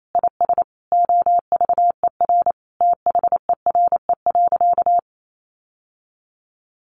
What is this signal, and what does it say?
Morse code, heard as a keyed tone, about 28 wpm, 700 Hz. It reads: IS OVER THERE.
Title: Sentences from Top 1000 Words 28wpm